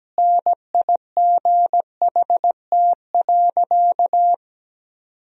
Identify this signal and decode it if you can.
Morse code, heard as a keyed tone, about 17 wpm, 700 Hz. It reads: NIGHT.
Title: Sentences from Top 500 Words and Abbreviations 17wpm